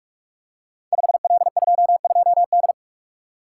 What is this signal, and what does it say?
Morse code, heard as a keyed tone, about 44 wpm, 700 Hz. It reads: 5L22D
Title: Code Groups - 5 Letter:Numbers - 45wpm